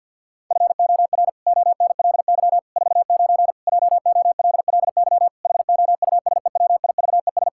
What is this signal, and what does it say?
Morse code, heard as a keyed tone, about 50 wpm, 700 Hz. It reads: FOR ONLY 49 JOLLY HOUSEWIVES
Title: Pangram Challenge v2 - 50wpm